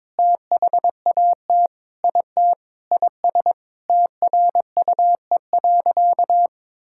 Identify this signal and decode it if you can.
Morse code, heard as a keyed tone, about 22 wpm, 700 Hz. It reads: THAT IT IS TRUE.
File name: practice / 22wpm / Sentences with 4-Word Phrases - 22wpm